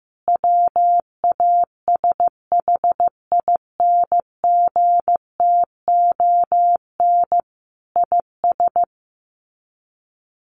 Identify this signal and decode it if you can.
Morse code, heard as a keyed tone, about 15 wpm, 700 Hz. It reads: WASHINGTON IS
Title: Sets of 5 Words Encore2 15wpm